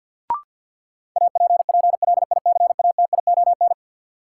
Morse code, heard as a keyed tone, about 53 wpm, 700 Hz. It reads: APPLICATION
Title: Sentences from Top 1200 Words - New Word Drill Alt - 50wpm